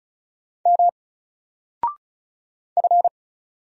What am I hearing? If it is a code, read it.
Morse code, heard as a keyed tone, about 35 wpm, 700 Hz. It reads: M F
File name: Single-Letter-Number-35wpm